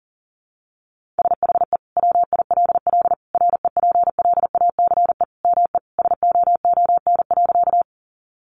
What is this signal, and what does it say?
Morse code, heard as a keyed tone, about 40 wpm, 700 Hz. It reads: SHE WILL REPLACE ME SOON.
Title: Sentences from Top 1400 Words 40wpm